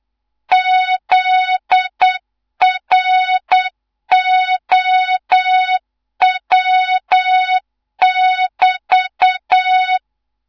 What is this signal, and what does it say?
Morse code, heard as a keyed tone, about 8 wpm, 3700 Hz, with noise. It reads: ZROW<BT>
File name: Morze-HA2MN / hang / 15 GYAKORLAT